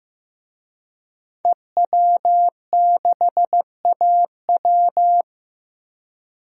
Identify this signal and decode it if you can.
Morse code, heard as a keyed tone, about 15 wpm, 700 Hz. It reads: EW6AW
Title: DXCC-100 Call Signs - 5 Characters - Repeated 2x - 15wpm